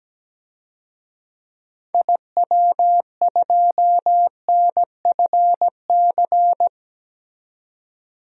Morse code, heard as a keyed tone, about 17 wpm, 700 Hz. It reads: IW2NFC
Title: DXCC-100 Call Signs - 6 Characters - 17wpm